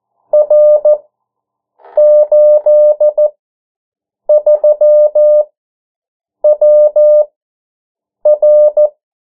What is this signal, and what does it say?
Morse code, heard as a keyed tone, about 14 wpm, spaced as for 7 wpm, 600 Hz. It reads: R83WR
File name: morse mp3 / 09 urok 8